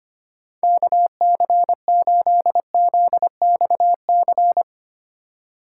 Morse code, heard as a keyed tone, about 25 wpm, 700 Hz. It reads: KC8ZXC
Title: USA Call Signs - 6 Characters - 3x Character Spacing - 25wpm